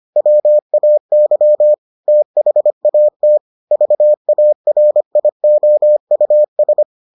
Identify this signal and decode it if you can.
Morse code, heard as a keyed tone, about 25 wpm, 600 Hz. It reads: WAY THAT VARIOUS